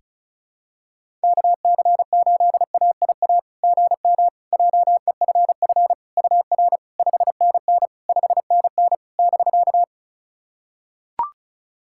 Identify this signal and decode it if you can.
Morse code, heard as a keyed tone, about 36 wpm, 690 Hz. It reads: KC8AIA GM JEFF UR 5NN 5NN <BK>
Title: SOTA QSO Practice 35wpm